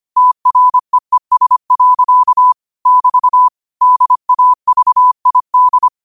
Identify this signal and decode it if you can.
Morse code, heard as a keyed tone, about 25 wpm, 1000 Hz. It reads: TREES. <BT> DAVID